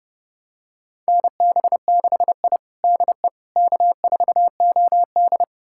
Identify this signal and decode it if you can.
Morse code, heard as a keyed tone, about 30 wpm, 700 Hz. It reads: NB6I DE K4OD
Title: USA Callsign DE Callsign - 4 Characters - 30wpm